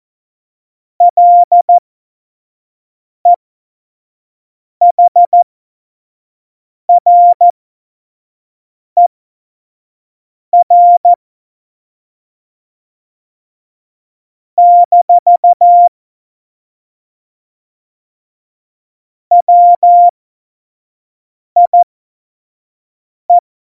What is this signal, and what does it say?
Morse code, heard as a keyed tone, about 14 wpm, spaced as for 5 wpm, 700 Hz. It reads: LEHRER - WIE
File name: Morsekurs / Witze / Witz 23. April 2019, 25 F70